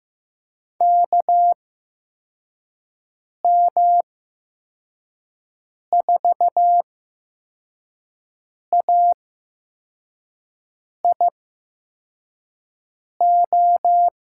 Morse code, heard as a keyed tone, about 15 wpm, 700 Hz. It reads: K M 4 A I O
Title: USA Call Signs - 6 Characters - 8x Character Spacing - 15wpm